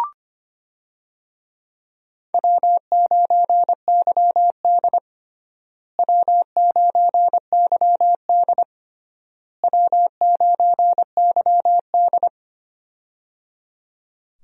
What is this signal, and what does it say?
Morse code, heard as a keyed tone, about 25 wpm, 700 Hz. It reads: W9YD W9YD W9YD
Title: USA Call Signs - 4 Characters - Repeated 3x - 25wpm